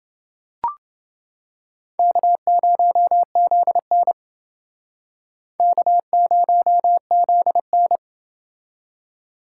Morse code, heard as a keyed tone, about 28 wpm, 700 Hz. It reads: K0ZN K0ZN